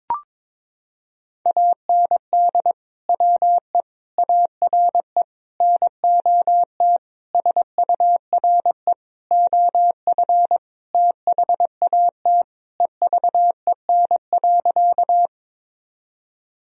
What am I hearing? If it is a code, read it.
Morse code, heard as a keyed tone, about 22 wpm, 700 Hz. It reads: AND WE ARE NOT SURE OF THAT EVEN.
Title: Sentences from Top 1000 Words - Review Part 1 - 22wpm